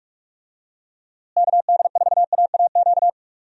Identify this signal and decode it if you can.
Morse code, heard as a keyed tone, about 46 wpm, 700 Hz. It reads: KD4AAQ
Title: USA Call Signs - 6 Characters - Repeated 3x - 45wpm